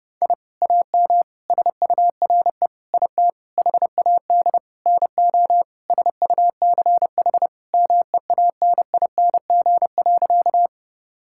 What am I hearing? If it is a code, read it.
Morse code, heard as a keyed tone, about 30 wpm, 700 Hz. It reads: I AM SURE IT HAD NO SUCH MEANING.
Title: Sentences from Top 1000 Words - Review Part 6 - 30wpm